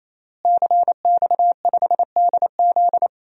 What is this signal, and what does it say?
Morse code, heard as a keyed tone, about 28 wpm, 700 Hz. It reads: CX5DZ
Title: DXCC-100 Call Signs - 5 Characters - Repeated 3x - 28wpm